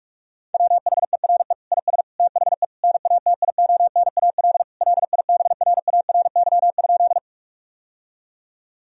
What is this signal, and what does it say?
Morse code, heard as a keyed tone, about 46 wpm, 700 Hz. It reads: WHERE IS THE NATIONAL LIBRARY?
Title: Sentences from Top 1000 Words - Review Part 4 - 45wpm